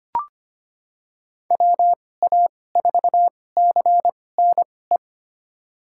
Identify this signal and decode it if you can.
Morse code, heard as a keyed tone, about 25 wpm, spaced as for 18 wpm, 700 Hz. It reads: WA4CNE
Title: USA Call Signs - 6 Characters - 2x Character Spacing - 25wpm